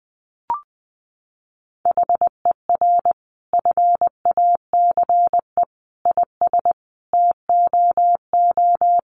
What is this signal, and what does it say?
Morse code, heard as a keyed tone, about 20 wpm, 700 Hz. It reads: HER FACE IS TOO
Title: Sentences from Top 1200 Words 20wpm